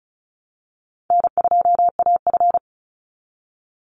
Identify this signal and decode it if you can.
Morse code, heard as a keyed tone, about 34 wpm, 700 Hz. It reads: N2AF
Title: NA SOTA Chaser Callsigns 35wpm